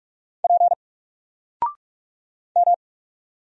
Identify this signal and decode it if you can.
Morse code, heard as a keyed tone, about 46 wpm, 700 Hz. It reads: P M
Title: Single Letter-Number 45wpm